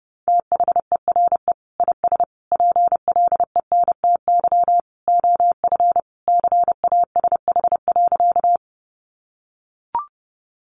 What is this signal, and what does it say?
Morse code, heard as a keyed tone, about 30 wpm, 700 Hz. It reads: THERE IS PLENTY OF CASH.
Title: Sentences from Top 1300 Words 30wpm